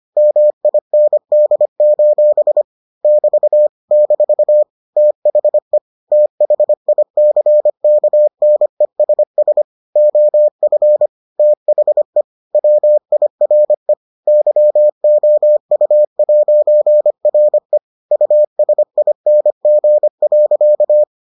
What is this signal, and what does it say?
Morse code, heard as a keyed tone, about 25 wpm, 600 Hz. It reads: MIND: <BT> - THE THICKNESS OF THE WIRE YOU'RE USING.